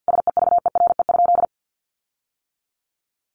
Morse code, heard as a keyed tone, about 50 wpm, 700 Hz. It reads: SEVERE?
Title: Sentences from Top 1400 Words 50wpm